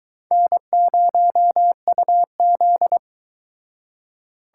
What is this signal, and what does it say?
Morse code, heard as a keyed tone, about 23 wpm, 700 Hz. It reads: N0UZ